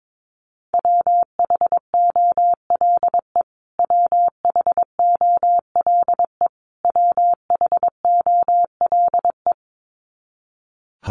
Morse code, heard as a keyed tone, about 22 wpm, 700 Hz. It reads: WHOLE WHOLE WHOLE
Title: Top 500 Words Repeated 22wpm